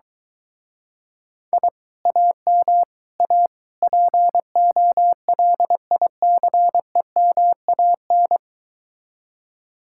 Morse code, heard as a keyed tone, about 23 wpm, 700 Hz. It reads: I AM A POLICEMAN